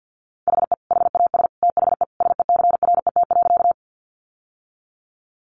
Morse code, heard as a keyed tone, about 50 wpm, 700 Hz. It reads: HE HAS THE SECRET.
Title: Sentences from Top 1200 Words 50wpm